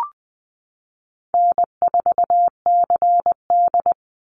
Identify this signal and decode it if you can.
Morse code, heard as a keyed tone, about 20 wpm, 700 Hz. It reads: N4CD
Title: Call Signs Repeated 2x - 20wpm